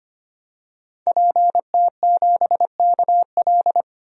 Morse code, heard as a keyed tone, about 25 wpm, 700 Hz. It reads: PT7KL
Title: DXCC-100 Call Signs - 5 Characters - 25wpm